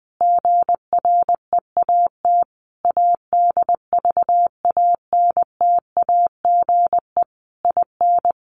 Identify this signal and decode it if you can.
Morse code, heard as a keyed tone, about 20 wpm, 700 Hz. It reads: GREAT ADVANTAGE IN
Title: Sentences from Top 900 Words 20wpm